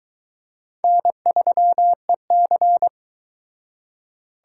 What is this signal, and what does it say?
Morse code, heard as a keyed tone, about 22 wpm, 700 Hz. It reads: N3EC